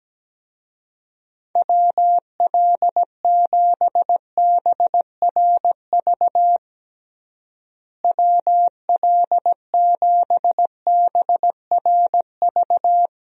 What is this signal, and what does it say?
Morse code, heard as a keyed tone, about 17 wpm, 700 Hz. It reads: WL7BRV WL7BRV
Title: USA Call Signs - 6 Characters - Repeated 2x - 17wpm